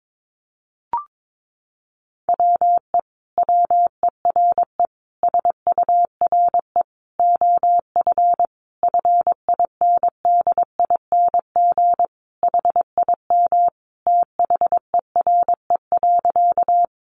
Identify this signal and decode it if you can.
Morse code, heard as a keyed tone, about 22 wpm, 700 Hz. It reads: WE WERE SURE OF FINDING HIM THERE.
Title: Sentences from Top 1100 Words 22wpm